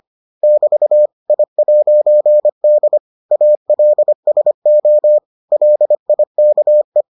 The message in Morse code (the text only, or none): <BT> I'D ALSO LIKE